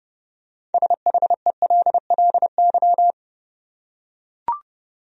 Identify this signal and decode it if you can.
Morse code, heard as a keyed tone, about 30 wpm, 700 Hz. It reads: SHELLY
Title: USA Common Names - 30wpm